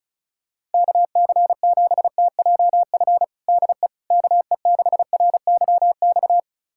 Morse code, heard as a keyed tone, about 36 wpm, 700 Hz. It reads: KC7TJF DE KE6RYX
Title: USA Callsign DE Callsign - 6 Characters - 35wpm